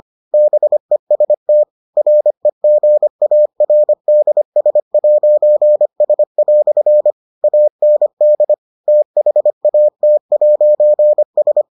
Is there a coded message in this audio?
Yes: Morse code from 0.3 to 11.6 s, heard as a keyed tone, about 25 wpm, 600 Hz. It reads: BEST REGARDS'S" AND THAT'S